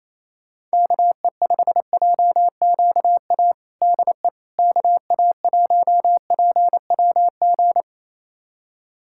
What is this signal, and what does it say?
Morse code, heard as a keyed tone, about 28 wpm, 700 Hz. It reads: KE5JQA DE KA1PWG
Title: USA Callsign DE Callsign 28wpm